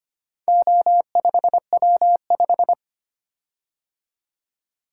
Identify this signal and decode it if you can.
Morse code, heard as a keyed tone, about 25 wpm, 700 Hz. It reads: O5W5
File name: practice / 25wpm / Four Letter-Number Combinations - 25wpm